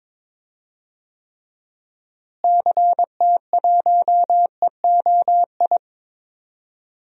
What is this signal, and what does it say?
Morse code, heard as a keyed tone, about 22 wpm, 700 Hz. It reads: CT1EOI
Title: DXCC-100 Call Signs - 6 Characters - Repeated 3x - 22wpm